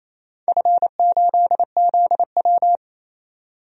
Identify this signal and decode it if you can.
Morse code, heard as a keyed tone, about 28 wpm, 700 Hz. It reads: F8ZW
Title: DXCC-100 Call Signs - 4 Characters - 28wpm